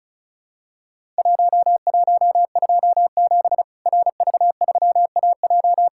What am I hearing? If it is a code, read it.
Morse code, heard as a keyed tone, about 36 wpm, 700 Hz. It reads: 1127 RV3AJ
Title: Worked All Europe DX Contest - QTC Exchange - 35wpm